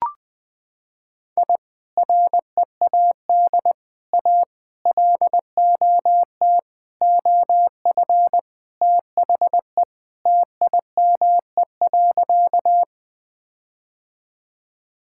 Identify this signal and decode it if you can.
Morse code, heard as a keyed tone, about 20 wpm, 700 Hz. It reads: I READ A LOT OF THE TIME.
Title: Sentences with 4-Word Phrases - 20wpm